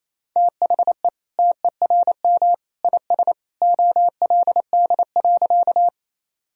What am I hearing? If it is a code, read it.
Morse code, heard as a keyed tone, about 28 wpm, 700 Hz. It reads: THE TERM IS OLD.
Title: Sentences from Top 1000 Words - Review Part 4 - 28wpm